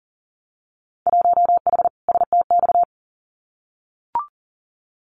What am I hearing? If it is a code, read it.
Morse code, heard as a keyed tone, about 41 wpm, 700 Hz. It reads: 1H STX
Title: Winter Field Day Exchanges 40wpm